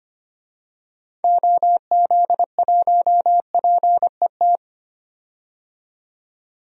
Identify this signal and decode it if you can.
Morse code, heard as a keyed tone, about 25 wpm, 700 Hz. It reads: OZ1PET
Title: DXCC-100 Call Signs - 6 Characters - 25wpm